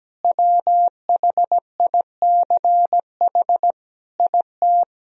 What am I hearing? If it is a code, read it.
Morse code, heard as a keyed tone, about 17 wpm, 700 Hz. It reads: WHICH IT
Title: Sentences from Top 1000 Words - Review Part 2 - 17wpm